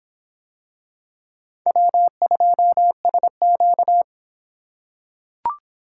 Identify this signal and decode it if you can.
Morse code, heard as a keyed tone, about 25 wpm, 700 Hz. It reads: W2SQ